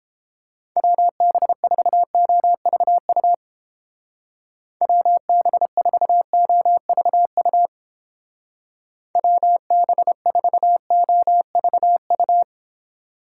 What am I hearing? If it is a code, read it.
Morse code, heard as a keyed tone, about 29 wpm, 700 Hz. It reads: WB4OVU WB4OVU WB4OVU